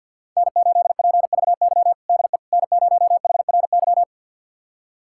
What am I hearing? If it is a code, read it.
Morse code, heard as a keyed tone, about 53 wpm, 700 Hz. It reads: N8PVY DE N0HRY
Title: USA Callsign DE Callsign - 5 Characters - 50wpm